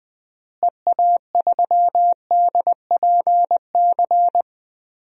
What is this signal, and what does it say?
Morse code, heard as a keyed tone, about 20 wpm, 700 Hz. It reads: EA3DPC
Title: DXCC-100 Call Signs - 6 Characters - Repeated 3x - 20wpm